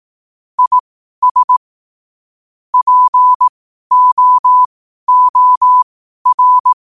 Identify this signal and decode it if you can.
Morse code, heard as a keyed tone, about 18 wpm, spaced as for 12 wpm, 1000 Hz. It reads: IS POOR
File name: qotdfiles / 12wpm